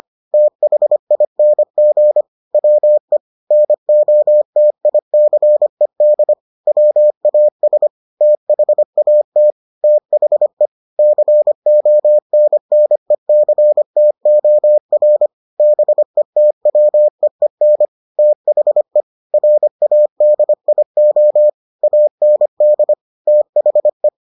Morse code, heard as a keyed tone, about 25 wpm, 600 Hz. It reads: THING WE NOTICED WAS THAT THE CONNECTOR BETWEEN THE RADIO AND THE